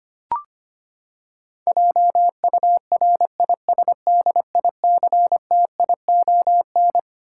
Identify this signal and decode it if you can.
Morse code, heard as a keyed tone, about 25 wpm, 700 Hz. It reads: JURISDICTION
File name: Twelve Letter Words - 25wpm